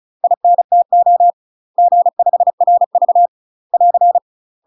Morse code, heard as a keyed tone, about 36 wpm, 700 Hz. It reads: INTO G5RV <AR>
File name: UK MASTER QSOs TRAINING / 35WPM / UK MASTER QSOs 0011